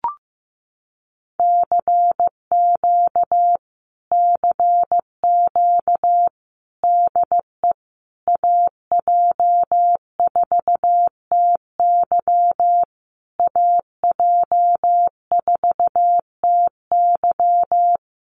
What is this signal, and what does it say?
Morse code, heard as a keyed tone, about 15 wpm, 700 Hz. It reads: CQ CQ DE AJ4TY AJ4TY
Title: Common QSO Phrases Encore 15wpm